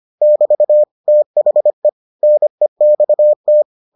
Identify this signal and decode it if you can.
Morse code, heard as a keyed tone, about 25 wpm, 600 Hz. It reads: <BT> THE NEXT